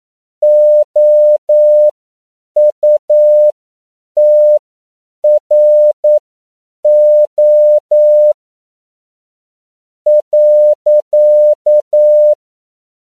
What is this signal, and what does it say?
Morse code, heard as a keyed tone, about 9 wpm, spaced as for 7 wpm, 600 Hz. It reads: OUTRO .
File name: CW / 10PPM